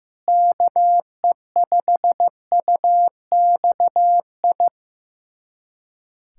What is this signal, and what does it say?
Morse code, heard as a keyed tone, about 15 wpm, 700 Hz. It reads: KE5UXI
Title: USA Callsign DE Callsign 15wpm